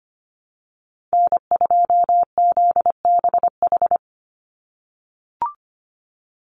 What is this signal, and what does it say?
Morse code, heard as a keyed tone, about 25 wpm, 700 Hz. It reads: N2ZBH
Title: USA Call Signs 25wpm